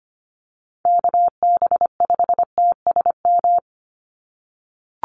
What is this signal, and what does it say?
Morse code, heard as a keyed tone, about 25 wpm, 700 Hz. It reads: KB5TSM
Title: USA Call Signs 25wpm